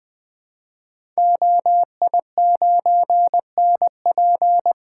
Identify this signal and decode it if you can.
Morse code, heard as a keyed tone, about 20 wpm, 700 Hz. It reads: OI9NP
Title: Code Groups - 5 Letter:Numbers - 20wpm